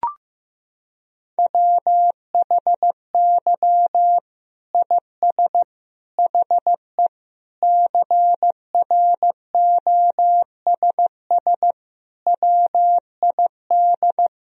Morse code, heard as a keyed tone, about 15 wpm, 700 Hz. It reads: WHY IS HE CROSS WID
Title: Sentences from Top 300 Words and Abbreviations 15wpm